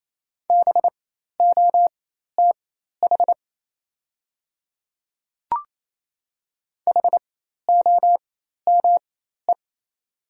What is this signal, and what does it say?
Morse code, heard as a keyed tone, about 28 wpm, spaced as for 13 wpm, 700 Hz. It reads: BOTH HOME
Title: 4x Character Spacing - Top 500 Words - 28wpm